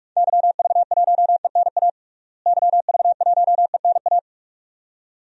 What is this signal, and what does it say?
Morse code, heard as a keyed tone, about 46 wpm, 700 Hz. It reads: YV1ENA YV1ENA
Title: DXCC-100 Call Signs - 6 Characters - Repeated 2x - 45wpm